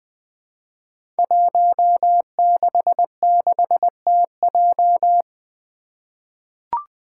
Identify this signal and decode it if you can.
Morse code, heard as a keyed tone, about 20 wpm, 700 Hz. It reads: 166TJ
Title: Five Challenging Letter-Number Combinations 20wpm